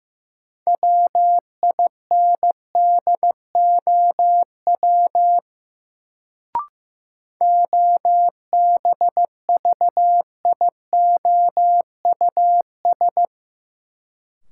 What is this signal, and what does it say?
Morse code, heard as a keyed tone, about 15 wpm, 700 Hz. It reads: WINDOW OBVIOUS
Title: Sentences from Top 1600 Words - New Word Drill Alt - 15wpm